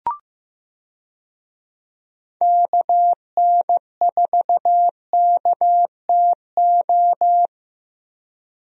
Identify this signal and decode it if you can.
Morse code, heard as a keyed tone, about 15 wpm, 700 Hz. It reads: KN4KTO
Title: USA Call Signs - 6 Characters - Repeated 3x - 15wpm